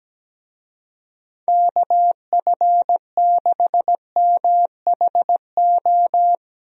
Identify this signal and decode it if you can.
Morse code, heard as a keyed tone, about 17 wpm, 700 Hz. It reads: KF6MHO
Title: USA Call Signs - 6 Characters - Repeated 2x - 17wpm